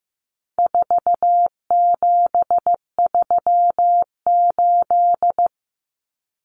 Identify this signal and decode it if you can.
Morse code, heard as a keyed tone, about 15 wpm, 700 Hz. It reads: 4738
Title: Numbers - 4-Digit 15wpm